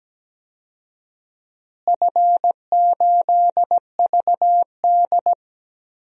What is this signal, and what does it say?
Morse code, heard as a keyed tone, about 17 wpm, 700 Hz. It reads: F8VD
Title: DXCC-100 Call Signs - 4 Characters - Repeated 3x - 17wpm